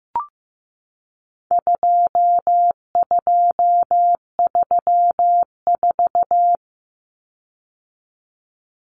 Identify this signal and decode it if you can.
Morse code, heard as a keyed tone, about 15 wpm, 700 Hz. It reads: 2234
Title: Common Cut Numbers - 4-Digit 15wpm